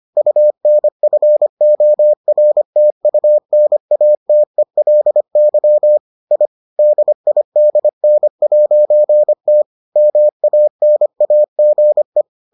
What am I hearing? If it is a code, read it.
Morse code, heard as a keyed tone, about 25 wpm, 600 Hz. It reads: UNFORTUNATELY I DIDN'T MANAGE